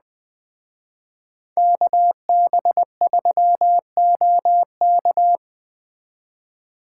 Morse code, heard as a keyed tone, about 20 wpm, 700 Hz. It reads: KB3OK